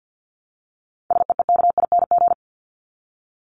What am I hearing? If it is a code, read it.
Morse code, heard as a keyed tone, about 50 wpm, 700 Hz. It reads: SEEKING
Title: ING Suffix - Top 100 Words with Base Words 50wpm